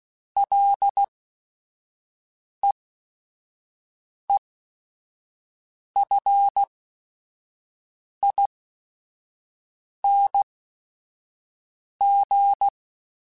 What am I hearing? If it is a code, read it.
Morse code, heard as a keyed tone, about 16 wpm, spaced as for 5 wpm, 800 Hz. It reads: LEEFING
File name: Ls / Single L m3s / Single L @5